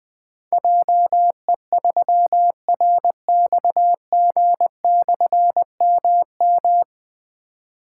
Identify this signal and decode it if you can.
Morse code, heard as a keyed tone, about 20 wpm, 700 Hz. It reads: JE3RXG/MM
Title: Difficult Call Signs 20wpm